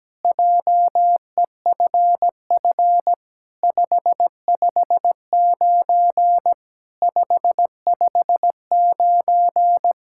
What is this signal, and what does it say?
Morse code, heard as a keyed tone, about 17 wpm, 700 Hz. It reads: JEFF 559 559
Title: SOTA QSO Practice 17wpm